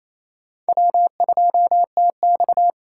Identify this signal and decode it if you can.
Morse code, heard as a keyed tone, about 28 wpm, 700 Hz. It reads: W2TX
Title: USA Call Signs - 4 Characters - 3x Character Spacing - 28wpm